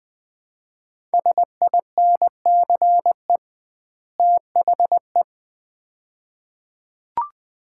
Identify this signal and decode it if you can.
Morse code, heard as a keyed tone, about 20 wpm, 700 Hz. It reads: SINCE THE
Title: 2x Word-Spacing - Sets of 2 Words - 20wpm